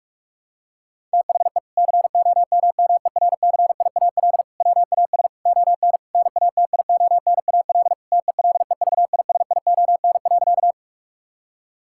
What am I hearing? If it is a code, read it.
Morse code, heard as a keyed tone, about 46 wpm, 700 Hz. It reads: THE COMMERCIAL WAS ON NATIONAL TELEVISION.
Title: Sentences from Top 1000 Words - Review Part 5 - 45wpm